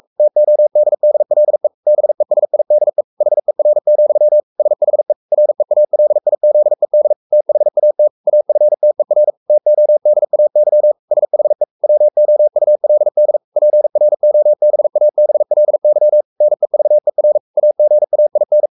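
Morse code, heard as a keyed tone, about 44 wpm, 600 Hz. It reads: TODDLE BESIDE HER, SHE REALIZED THAT AFTER TODAY SHE WOULD PROBABLY NEVER AGAIN
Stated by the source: Morse code at 43 wpm sending Ken Follett's novel Never in English, from CW med Gnister.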